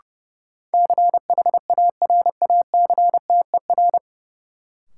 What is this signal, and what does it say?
Morse code, heard as a keyed tone, about 32 wpm, 700 Hz. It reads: CHARACTER
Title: Sentences from Top 1200 Words - New Word Drill - 30wpm